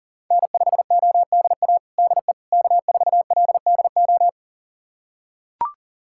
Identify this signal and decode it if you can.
Morse code, heard as a keyed tone, about 37 wpm, 700 Hz. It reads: N5ODA DE K4LDO
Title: USA Callsign DE Callsign - 5 Characters - 40wpm